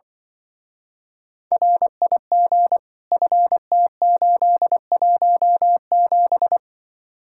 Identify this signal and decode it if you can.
Morse code, heard as a keyed tone, about 25 wpm, 700 Hz. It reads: RIG FT817
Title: Speed-Racing Common QSO Phrases - 20wpm